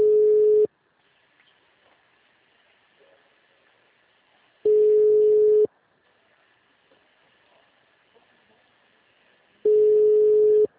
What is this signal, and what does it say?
Ringback tone (Europe, 425 Hz).